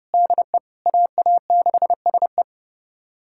Morse code, heard as a keyed tone, about 30 wpm, 700 Hz. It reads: DE AA6SE
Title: USA Callsign DE Callsign - 5 Characters - 30wpm